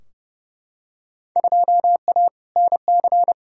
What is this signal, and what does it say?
Morse code, heard as a keyed tone, about 31 wpm, 700 Hz. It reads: 2A NC
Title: Field Day Exchanges Encore - 30wpm